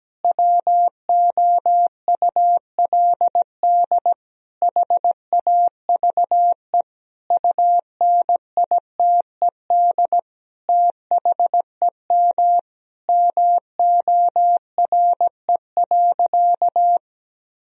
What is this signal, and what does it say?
Morse code, heard as a keyed tone, about 17 wpm, 700 Hz. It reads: WOULD HAVE UNITED THEM MORE.
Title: Sentences from Top 1000 Words - Review Part 4 - 17wpm